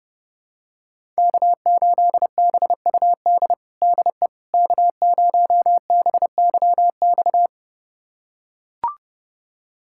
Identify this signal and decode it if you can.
Morse code, heard as a keyed tone, about 30 wpm, 700 Hz. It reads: K8BUD DE K0BYX
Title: USA Callsign DE Callsign - 5 Characters - 30wpm